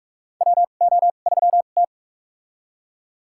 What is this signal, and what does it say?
Morse code, heard as a keyed tone, about 44 wpm, spaced as for 32 wpm, 700 Hz. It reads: WO3T
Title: USA Call Signs - 4 Characters - 2x Character Spacing - 45wpm